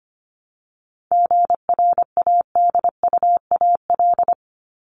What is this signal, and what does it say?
Morse code, heard as a keyed tone, about 25 wpm, 700 Hz. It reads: GRADUAL
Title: LY Suffix - Top 100 Words with Base Words 25wpm